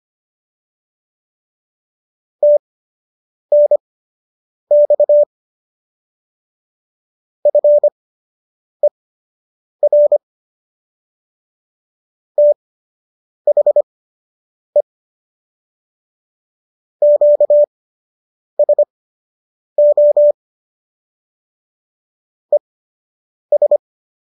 Morse code, heard as a keyed tone, about 25 wpm, spaced as for 8 wpm, 590 Hz. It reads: TNX FER THE QSO ES